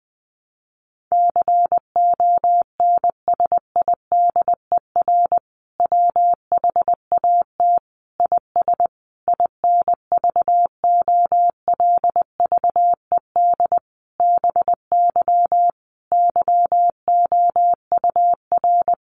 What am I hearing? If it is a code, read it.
Morse code, heard as a keyed tone, about 20 wpm, 700 Hz. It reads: CONSIDER WHAT IS INVOLVED BY YOUR
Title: Sentences from Top 800 Words 20wpm